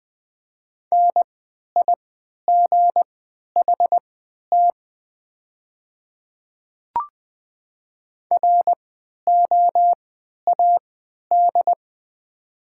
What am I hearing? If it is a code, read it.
Morse code, heard as a keyed tone, about 20 wpm, spaced as for 11 wpm, 700 Hz. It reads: NIGHT ROAD
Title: 3x Character Spacing - Top 500 Words - 20wpm